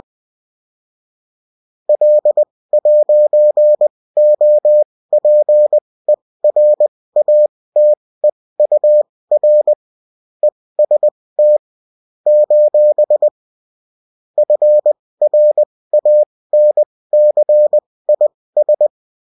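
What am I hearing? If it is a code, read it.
Morse code, heard as a keyed tone, about 20 wpm, spaced as for 16 wpm, 600 Hz. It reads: L'OPERATEUR EST : FRANCIS
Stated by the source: Morse code bulletin of a French amateur CW union, sent as text and in CW.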